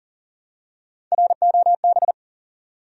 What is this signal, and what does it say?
Morse code, heard as a keyed tone, about 40 wpm, 700 Hz. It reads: ROB
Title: USA Common Names - Max 3 Letters - 40wpm